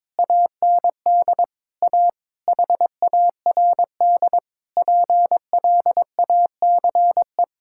Morse code, heard as a keyed tone, about 22 wpm, 700 Hz. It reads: AND A HARD PLACE
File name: practice / 22wpm / Sets of 8 Words - 22wpm